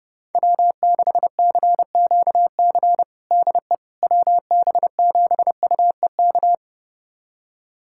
Morse code, heard as a keyed tone, about 30 wpm, 700 Hz. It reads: W6CQC DE WB7UEK
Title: USA Callsign DE Callsign 30wpm